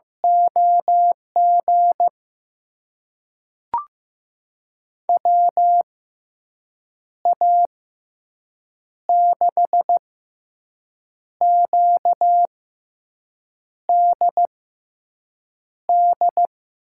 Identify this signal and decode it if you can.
Morse code, heard as a keyed tone, about 15 wpm, spaced as for 6 wpm, 700 Hz. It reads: OG WA6QDD
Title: USA Call Signs - 6 Characters - 6x Character Spacing - 15wpm